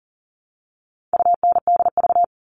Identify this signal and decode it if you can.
Morse code, heard as a keyed tone, about 40 wpm, 700 Hz. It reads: UNDV
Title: All Cut Numbers - 4-Digit 40wpm